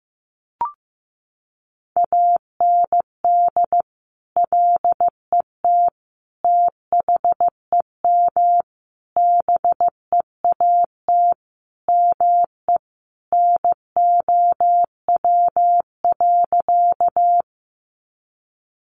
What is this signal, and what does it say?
Morse code, heard as a keyed tone, about 15 wpm, 700 Hz. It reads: AND LET THEM BEAT ME NOW.
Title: Sentences from Top 1300 Words 15wpm